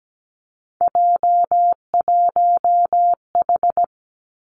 Morse code, heard as a keyed tone, about 17 wpm, 700 Hz. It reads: J1H
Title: Three Challenging Letter-Number Combinations 17wpm